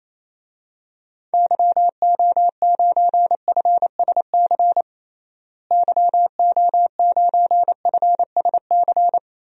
Morse code, heard as a keyed tone, about 28 wpm, 700 Hz. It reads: YO9FSC YO9FSC
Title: DXCC-100 Call Signs - 6 Characters - Repeated 2x - 28wpm